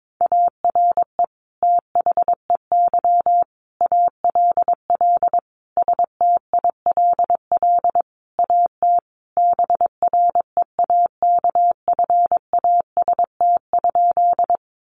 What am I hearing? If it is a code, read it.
Morse code, heard as a keyed tone, about 22 wpm, 700 Hz. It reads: ARE THEY ALL STILL AT BREAKFAST?
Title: Sentences from Top 1500 Words 22wpm